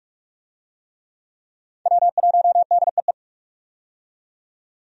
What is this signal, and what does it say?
Morse code, heard as a keyed tone, about 42 wpm, 700 Hz. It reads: W1DEE
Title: USA Call Signs - 5 Characters - 45wpm